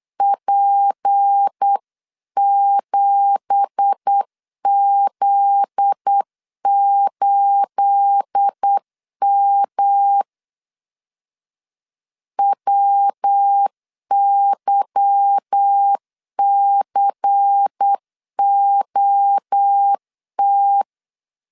Morse code, heard as a keyed tone, about 8 wpm, 800 Hz. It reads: P7Z8M WYCOT
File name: 8wpm